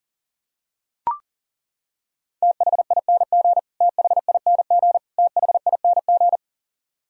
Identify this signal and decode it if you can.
Morse code, heard as a keyed tone, about 37 wpm, 700 Hz. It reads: THING THING THING
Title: Top 100 Words Repeated 40wpm